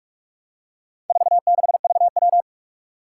Morse code, heard as a keyed tone, about 44 wpm, 700 Hz. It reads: 46VW
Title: Four Challenging Letter-Number Combinations 45wpm